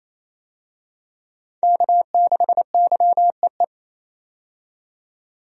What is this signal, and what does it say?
Morse code, heard as a keyed tone, about 28 wpm, 700 Hz. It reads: K6YEE
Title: USA Call Signs - 5 Characters - 28wpm